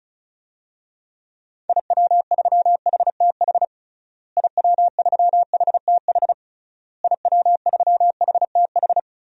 Morse code, heard as a keyed tone, about 34 wpm, 700 Hz. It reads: IW3HTH IW3HTH IW3HTH
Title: DXCC-100 Call Signs - 6 Characters - Repeated 3x - 35wpm